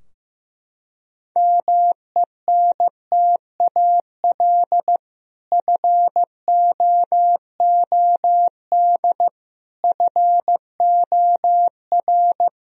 Morse code, heard as a keyed tone, about 15 wpm, 700 Hz. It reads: MENTAL FOOD FOR
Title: Sentences from Top 1000 Words - Review Part 4 - 15wpm